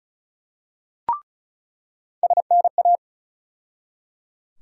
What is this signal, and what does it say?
Morse code, heard as a keyed tone, about 34 wpm, 700 Hz. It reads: SNA
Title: USA Call Sign Postfixes - 3 Letters 35wpm